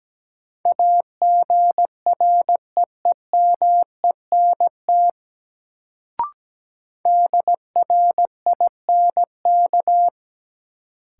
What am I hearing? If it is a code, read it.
Morse code, heard as a keyed tone, about 17 wpm, 700 Hz. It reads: AGREEMENT DRINK
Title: Sentences from Top 1500 Words - New Word Drill - 17wpm